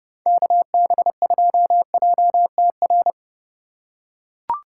Morse code, heard as a keyed tone, about 32 wpm, 700 Hz. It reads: KB2JTR